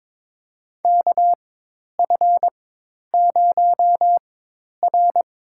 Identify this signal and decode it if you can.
Morse code, heard as a keyed tone, about 22 wpm, spaced as for 10 wpm, 700 Hz. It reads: KF0R
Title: USA Call Signs - 4 Characters - 4x Character Spacing - 22wpm